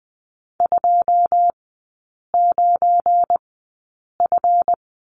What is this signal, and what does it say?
Morse code, heard as a keyed tone, about 20 wpm, spaced as for 8 wpm, 690 Hz. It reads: 29F
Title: 172 Lesson 28e - Number 9 - Three Characters 20wpm